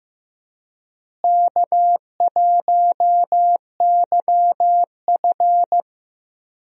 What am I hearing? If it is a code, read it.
Morse code, heard as a keyed tone, about 15 wpm, 700 Hz. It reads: K1YF
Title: USA Call Signs - 4 Characters - Repeated 3x - 15wpm